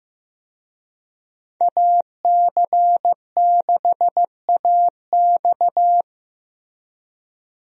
Morse code, heard as a keyed tone, about 15 wpm, 700 Hz. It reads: AC6AX
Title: USA Call Signs - 5 Characters - Repeated 3x - 15wpm